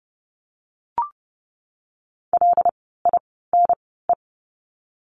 Morse code, heard as a keyed tone, about 30 wpm, spaced as for 17 wpm, 700 Hz. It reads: LINE
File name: Lesson 17f - Letter u - Words 30wpm